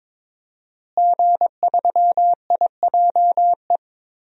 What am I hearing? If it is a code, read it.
Morse code, heard as a keyed tone, about 22 wpm, 700 Hz. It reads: G3IJE
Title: DXCC-100 Call Signs - 5 Characters - Repeated 2x - 22wpm